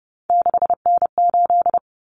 Morse code, heard as a keyed tone, about 30 wpm, 700 Hz. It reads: 6N8
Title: Common Cut Numbers - 3-Digit 30wpm